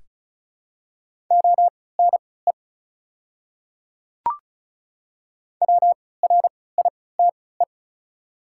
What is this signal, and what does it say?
Morse code, heard as a keyed tone, about 36 wpm, spaced as for 20 wpm, 700 Hz. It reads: ONE WRITE
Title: 3x Character Spacing - Top 500 Words - 35wpm